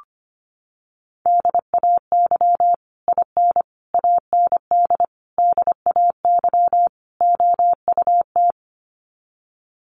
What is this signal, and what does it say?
Morse code, heard as a keyed tone, about 25 wpm, 700 Hz. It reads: DAY IN AND DAY OUT
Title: Sets of 5 Words Encore2 25wpm